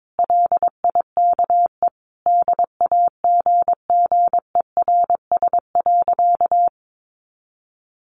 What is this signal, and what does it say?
Morse code, heard as a keyed tone, about 22 wpm, 700 Hz. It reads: LIKE DAGGERS.
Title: A Sound Of Thunder 22wpm